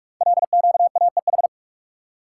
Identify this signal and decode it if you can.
Morse code, heard as a keyed tone, about 44 wpm, 700 Hz. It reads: LQAEH
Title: Code Groups - 5 Letters - 45wpm